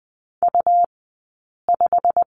Morse code, heard as a keyed tone, about 20 wpm, 700 Hz. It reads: U 5
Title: 190 Lesson 31d - Letter v - Two Characters 20wpm